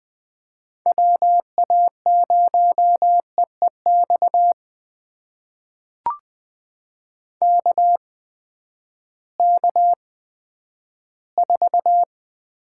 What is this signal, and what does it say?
Morse code, heard as a keyed tone, about 20 wpm, spaced as for 11 wpm, 700 Hz. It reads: WA0EEX KK4
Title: USA Call Signs - 6 Characters - 8x Character Spacing - 20wpm